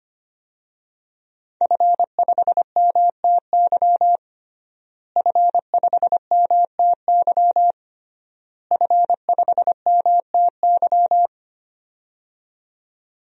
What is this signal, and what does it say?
Morse code, heard as a keyed tone, about 25 wpm, 700 Hz. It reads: F5MTY F5MTY F5MTY
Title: DXCC-100 Call Signs - 5 Characters - Repeated 3x - 25wpm